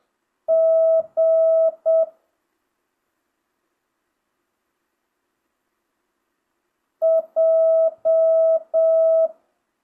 Morse code, heard as a keyed tone, about 7 wpm, 650 Hz. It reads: G J